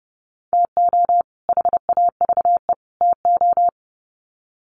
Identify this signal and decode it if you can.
Morse code, heard as a keyed tone, about 30 wpm, 700 Hz. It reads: TO HAVE TO
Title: Sets of 3 Words 30wpm